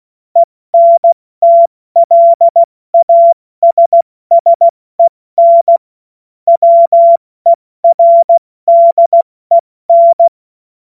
Morse code, heard as a keyed tone, about 16 wpm, spaced as for 14 wpm, 680 Hz. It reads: ENTLASSEN WERDEN